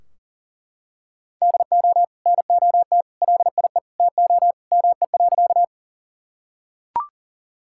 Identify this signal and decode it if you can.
Morse code, heard as a keyed tone, about 40 wpm, 700 Hz. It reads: DO NOT LIE TO ME.
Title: Sentences from Top 1600 Words 40wpm